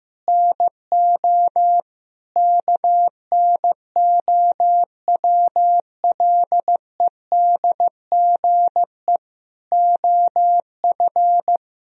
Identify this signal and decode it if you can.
Morse code, heard as a keyed tone, about 15 wpm, 700 Hz. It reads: NO KNOWLEDGE OF
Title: Sentences from Top 1000 Words - Review Part 3 - 15wpm